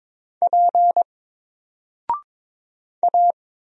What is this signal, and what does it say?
Morse code, heard as a keyed tone, about 22 wpm, 700 Hz. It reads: P A
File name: Single-Letter-Number-22wpm